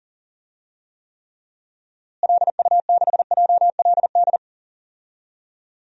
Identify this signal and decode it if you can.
Morse code, heard as a keyed tone, about 37 wpm, 700 Hz. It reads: LU6JLD
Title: DXCC-100 Call Signs - 6 Characters - Repeated 3x - 40wpm